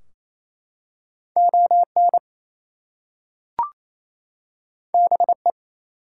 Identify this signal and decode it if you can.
Morse code, heard as a keyed tone, about 27 wpm, 700 Hz. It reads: ON BE
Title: Two Letter Words - 28wpm